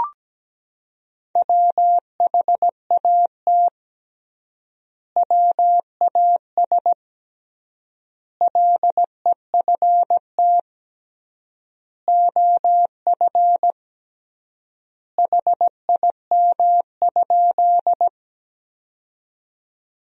Morse code, heard as a keyed tone, about 17 wpm, 700 Hz. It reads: WHAT WAS LEFT OF HIM?
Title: 3x Word-Spacing - Sentences from Top 300 Words - 17wpm